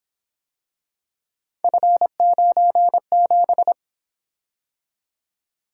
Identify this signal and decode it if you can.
Morse code, heard as a keyed tone, about 27 wpm, 700 Hz. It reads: F97
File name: Speed-Racing - Three Letter-Number Combinations - 22wpm